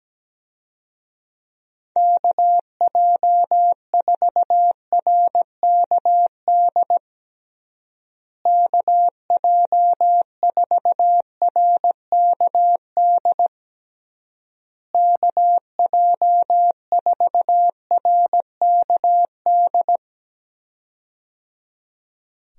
Morse code, heard as a keyed tone, about 17 wpm, 700 Hz. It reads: KJ4RKD KJ4RKD KJ4RKD
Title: USA Call Signs - 6 Characters - Repeated 3x - 17wpm